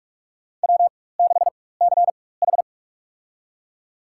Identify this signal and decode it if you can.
Morse code, heard as a keyed tone, about 46 wpm, spaced as for 21 wpm, 700 Hz. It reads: W6CH
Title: USA Call Signs - 4 Characters - 4x Character Spacing - 45wpm